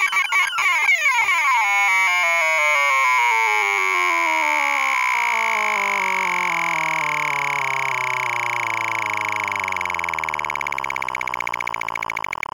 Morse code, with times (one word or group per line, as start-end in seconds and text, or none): none